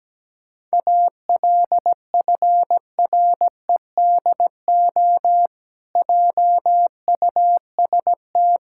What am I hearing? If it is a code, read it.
Morse code, heard as a keyed tone, about 17 wpm, 700 Hz. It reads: ALFREDO JUST
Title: Pangram Challenge v2 - 17wpm